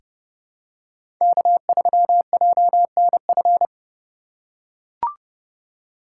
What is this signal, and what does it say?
Morse code, heard as a keyed tone, about 30 wpm, 700 Hz. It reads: K3JNF
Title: USA Call Signs - 4x Character Spacing - 30wpm